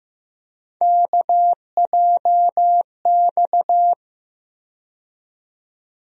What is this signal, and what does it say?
Morse code, heard as a keyed tone, about 15 wpm, 700 Hz. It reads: KJX
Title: USA Call Sign Postfixes - 3 Letters 15wpm